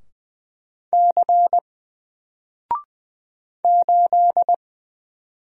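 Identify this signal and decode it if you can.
Morse code, heard as a keyed tone, about 20 wpm, 700 Hz. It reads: C 8
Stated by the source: Single letter and number Morse drill at 20 wpm, from N4LCD.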